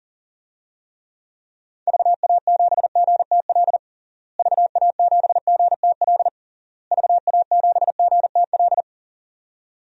Morse code, heard as a keyed tone, about 37 wpm, 700 Hz. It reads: VA7GTL VA7GTL VA7GTL
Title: DXCC-100 Call Signs - 6 Characters - Repeated 3x - 40wpm